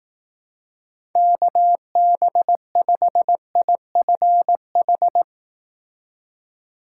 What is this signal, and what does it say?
Morse code, heard as a keyed tone, about 18 wpm, 700 Hz. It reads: KB5IFH
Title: Speed Racing - USA 2x3 Call Signs 15wpm